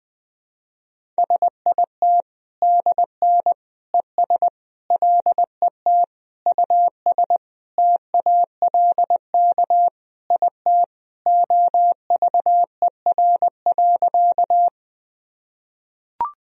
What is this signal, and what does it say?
Morse code, heard as a keyed tone, about 20 wpm, 700 Hz. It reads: SIT DN ES LET US TALK IT OVER.
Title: Sentences from Top 500 Words and Abbreviations 20wpm